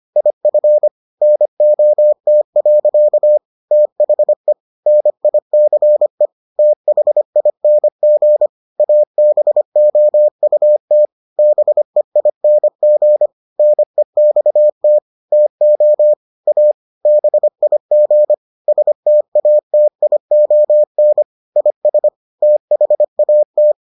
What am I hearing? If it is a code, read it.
Morse code, heard as a keyed tone, about 25 wpm, 600 Hz. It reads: IF NOT. THE NICE THING ABOUT BEING NEXT TO A BIG STATION IS THAT